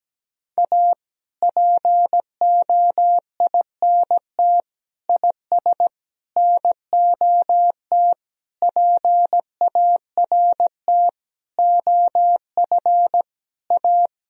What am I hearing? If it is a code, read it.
Morse code, heard as a keyed tone, about 17 wpm, 700 Hz. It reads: A POINT IS NOT PART OF A
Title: Sentences from Top 1000 Words - Review Part 1 - 17wpm